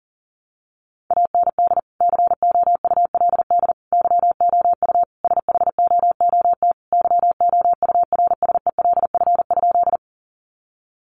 Morse code, heard as a keyed tone, about 40 wpm, 700 Hz. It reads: AND COULD YOU SHOOT YOURSELF?
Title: Sentences from Top 1200 Words 40wpm